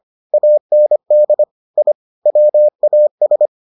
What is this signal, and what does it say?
Morse code, heard as a keyed tone, about 25 wpm, 600 Hz. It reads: AND I WAS